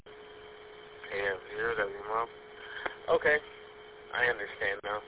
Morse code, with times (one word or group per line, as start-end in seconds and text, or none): none